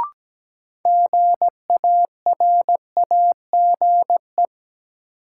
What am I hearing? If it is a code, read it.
Morse code, heard as a keyed tone, about 17 wpm, 700 Hz. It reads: GARAGE
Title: Sentences from Top 1600 Words - New Word Drill - 17wpm